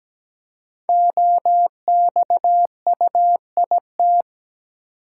Morse code, heard as a keyed tone, about 17 wpm, 700 Hz. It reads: OXUIT
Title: Code Groups - 5 Letters - 17wpm